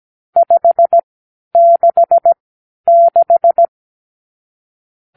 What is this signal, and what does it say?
Morse code, heard as a keyed tone, about 17 wpm, spaced as for 10 wpm, 680 Hz. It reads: 566
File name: k7qo / Lesson 52 (8)